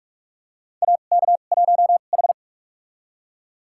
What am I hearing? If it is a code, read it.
Morse code, heard as a keyed tone, about 45 wpm, spaced as for 32 wpm, 700 Hz. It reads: AK1H
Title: USA Call Signs - 2x Character Spacing - 45wpm